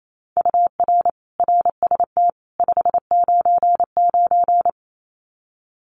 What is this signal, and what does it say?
Morse code, heard as a keyed tone, about 28 wpm, 700 Hz. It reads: UR RST 599
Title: Common QSO Phrases 28wpm